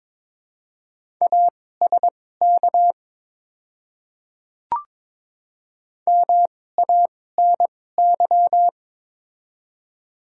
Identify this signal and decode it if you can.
Morse code, heard as a keyed tone, about 22 wpm, spaced as for 16 wpm, 700 Hz. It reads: ASK MANY
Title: 2x Character Spacing - Top 200 Words - 22wpm